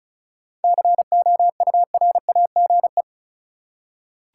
Morse code, heard as a keyed tone, about 34 wpm, 700 Hz. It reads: COURAGE
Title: Seven Letter Words - 35wpm